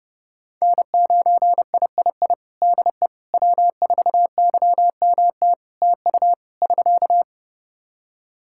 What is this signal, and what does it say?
Morse code, heard as a keyed tone, about 31 wpm, 700 Hz. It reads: N9III DE W4YMT TU <SK>
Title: Practice QSO - Exchange 4 - Ending - 30wpm